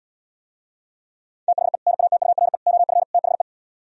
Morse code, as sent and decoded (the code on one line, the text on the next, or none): - .... / -. .- - ..- .-. / --- ..-. / - .... .
THE NATURE OF THE